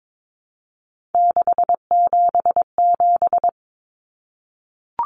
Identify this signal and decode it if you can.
Morse code, heard as a keyed tone, about 22 wpm, 700 Hz. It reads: 677
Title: Numbers - 3-Digit 22wpm